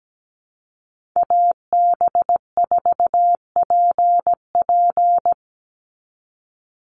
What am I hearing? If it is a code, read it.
Morse code, heard as a keyed tone, about 17 wpm, 700 Hz. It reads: AB4PP
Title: NA SOTA Chaser Callsigns 17wpm